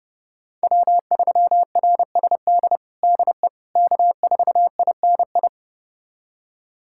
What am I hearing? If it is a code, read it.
Morse code, heard as a keyed tone, about 32 wpm, 700 Hz. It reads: W3RSD DE K4INI
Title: USA Callsign DE Callsign - 5 Characters - 30wpm